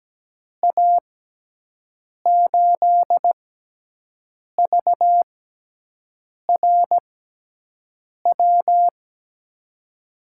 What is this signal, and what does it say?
Morse code, heard as a keyed tone, about 17 wpm, spaced as for 6 wpm, 700 Hz. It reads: A8VRW
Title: USA Call Signs - 6 Characters - 6x Character Spacing - 17wpm